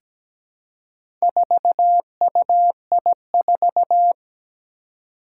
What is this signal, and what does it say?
Morse code, heard as a keyed tone, about 17 wpm, 700 Hz. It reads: 4UI4
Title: Four Challenging Letter-Number Combinations 17wpm